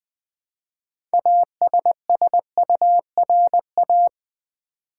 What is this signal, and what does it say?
Morse code, heard as a keyed tone, about 20 wpm, 700 Hz. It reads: ASSURA
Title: Build A Word - 3 Letters At A Time - 9-Letter Words - 20wpm